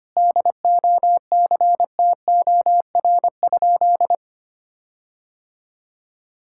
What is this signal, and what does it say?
Morse code, heard as a keyed tone, about 25 wpm, 700 Hz. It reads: DOCTOR?
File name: practice / 25wpm / Sentences with 4-Word Phrases - 25wpm